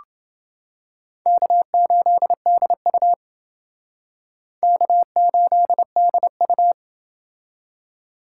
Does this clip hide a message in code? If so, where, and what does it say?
Yes: Morse code from 1.3 to 6.7 s, heard as a keyed tone, about 28 wpm, 700 Hz. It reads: K8DU K8DU